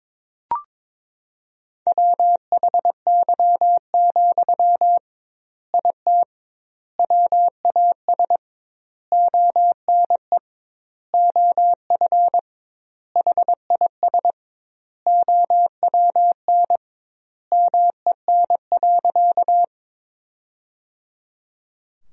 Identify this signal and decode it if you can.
Morse code, heard as a keyed tone, about 22 wpm, 700 Hz. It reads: WHY, IT WAS ONE OF HIS OWN MEN.
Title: 2x Word-Spacing - Sentences from Top 300 Words - 22wpm